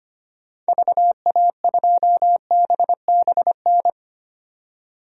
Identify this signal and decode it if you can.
Morse code, heard as a keyed tone, about 25 wpm, 700 Hz. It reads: VA2BBN
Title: DXCC-100 Call Signs - 6 Characters - 25wpm